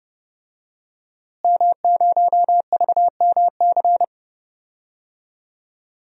Morse code, heard as a keyed tone, about 30 wpm, 700 Hz. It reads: M0VMC
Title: DXCC-100 Call Signs - 5 Characters - Repeated 2x - 30wpm